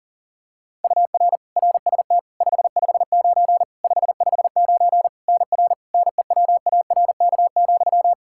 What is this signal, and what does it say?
Morse code, heard as a keyed tone, about 40 wpm, 700 Hz. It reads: UR RST 559 559 NR NEWARK,
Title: K1USN SST QSO Practice 40wpm